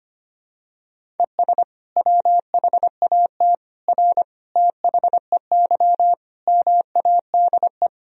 Morse code, heard as a keyed tone, about 25 wpm, 700 Hz. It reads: ES WHAT R THEY MADE
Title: Sentences from Top 200 Words and Abbreviations 25wpm